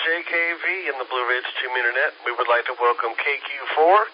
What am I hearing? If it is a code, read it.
Emergency Alert System (SAME) tones, heard as harsh buzzing bursts, with speech. It is heard from 0.2 to 3.6 s.